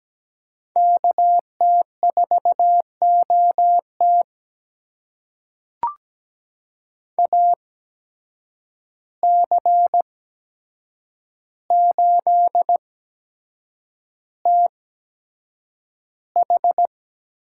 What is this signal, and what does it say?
Morse code, heard as a keyed tone, about 17 wpm, spaced as for 7 wpm, 700 Hz. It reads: KT4OT AC8TH
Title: USA Call Signs - 5 Characters - 8x Character Spacing - 17wpm